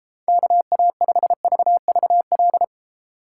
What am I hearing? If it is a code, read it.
Morse code, heard as a keyed tone, about 29 wpm, 690 Hz. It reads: KA5VVL